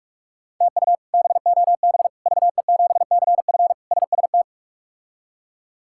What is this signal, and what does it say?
Morse code, heard as a keyed tone, about 42 wpm, 700 Hz. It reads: TU BOB VE7CF SST
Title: K1USN SST QSO Practice - 45wpm